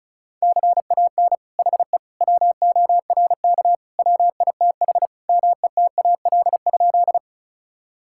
Morse code, heard as a keyed tone, about 36 wpm, 700 Hz. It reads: CAN HE WORK WITH METAL?
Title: Sentences from Top 1000 Words - Review Part 6 - 35wpm